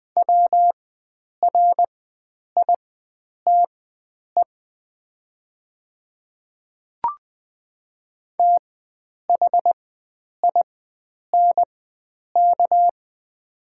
Morse code, heard as a keyed tone, about 20 wpm, spaced as for 9 wpm, 700 Hz. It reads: WRITE THINK
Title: 4x Character Spacing - Top 500 Words - 20wpm